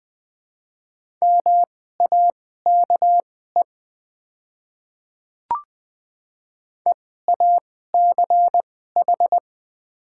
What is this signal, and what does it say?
Morse code, heard as a keyed tone, about 20 wpm, spaced as for 14 wpm, 700 Hz. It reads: MAKE EACH
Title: 2x Character Spacing - Top 300 Words - 20wpm